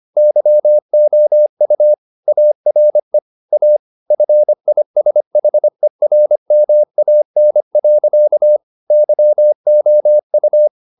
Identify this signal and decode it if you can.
Morse code, heard as a keyed tone, about 25 wpm, 600 Hz. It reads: YOU ARE A FISHERMAN. YOU